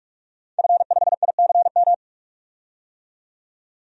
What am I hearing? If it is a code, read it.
Morse code, heard as a keyed tone, about 44 wpm, 690 Hz. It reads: F5ICM
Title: DXCC-100 Call Signs - 5 Characters - 45wpm